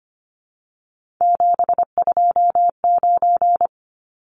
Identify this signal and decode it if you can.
Morse code, heard as a keyed tone, about 25 wpm, 700 Hz. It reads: 729
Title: Numbers - 3-Digit 25wpm